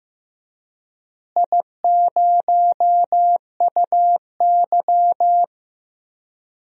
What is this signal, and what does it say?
Morse code, heard as a keyed tone, about 15 wpm, 700 Hz. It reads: I0UY
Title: DXCC-100 Call Signs - 4 Characters - Repeated 3x - 15wpm